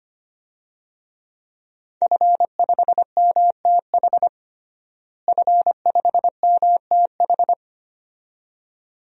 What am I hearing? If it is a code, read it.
Morse code, heard as a keyed tone, about 25 wpm, 700 Hz. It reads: F5MTH F5MTH
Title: DXCC-100 Call Signs - 5 Characters - Repeated 2x - 25wpm